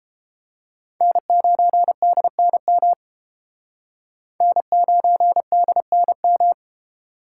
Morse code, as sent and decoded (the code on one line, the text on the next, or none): -. ----. -.. -. -- / -. ----. -.. -. --
N9DNM N9DNM